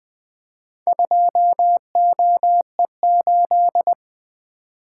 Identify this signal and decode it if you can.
Morse code, heard as a keyed tone, about 21 wpm, 700 Hz. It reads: 2OE8
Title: Four Letter-Number Combinations - 20wpm